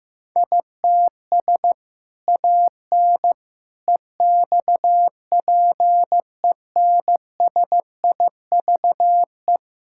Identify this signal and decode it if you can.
Morse code, heard as a keyed tone, about 15 wpm, 700 Hz. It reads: ITS AN EXPENSIVE
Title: Sentences from Top 1000 Words - Review Part 6 - 15wpm